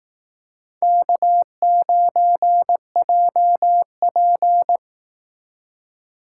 Morse code, heard as a keyed tone, about 18 wpm, 700 Hz. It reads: K9JP
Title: Speed Racing - USA Call Signs 15wpm